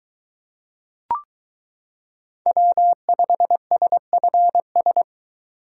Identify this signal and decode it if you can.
Morse code, heard as a keyed tone, about 22 wpm, 700 Hz. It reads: W5SFS